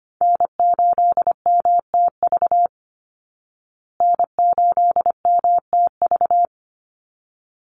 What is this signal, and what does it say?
Morse code, heard as a keyed tone, about 25 wpm, 700 Hz. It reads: N8MTV N8MTV
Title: Call Signs Repeated 3x - 25wpm